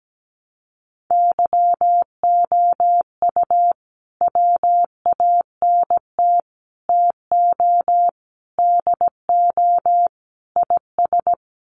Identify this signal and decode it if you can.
Morse code, heard as a keyed tone, about 17 wpm, 700 Hz. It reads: YOU WANT TO DO IS
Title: Sets of 5 Words Encore2 17wpm